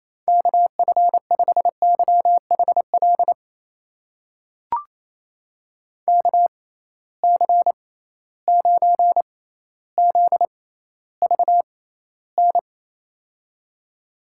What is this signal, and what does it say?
Morse code, heard as a keyed tone, about 28 wpm, spaced as for 14 wpm, 700 Hz. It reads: KF5YHL KC9ZVN
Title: USA Call Signs - 6 Characters - 6x Character Spacing - 28wpm